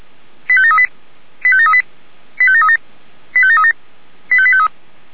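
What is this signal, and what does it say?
Five-tone selective call (ZVEI), heard as quick runs of five tones, tones of about 70 ms. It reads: 85738 85738 85737 85737 85853
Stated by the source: ZVEI five-tone selective call received on an SDR.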